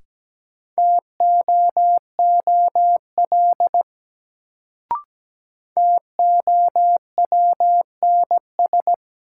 Morse code, heard as a keyed tone, about 17 wpm, 700 Hz. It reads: TOOL TOWNS
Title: Sentences from Top 1000 Words - Review Part 6 - New Word Drill - 17wpm